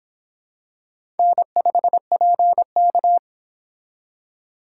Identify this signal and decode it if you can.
Morse code, heard as a keyed tone, about 25 wpm, 700 Hz. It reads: N5PK